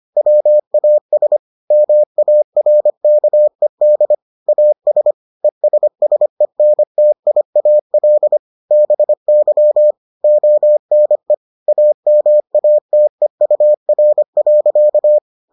Morse code, heard as a keyed tone, about 25 wpm, 600 Hz. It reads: WAS MARKED AS ESSENTIAL BY ONE AMATEUR.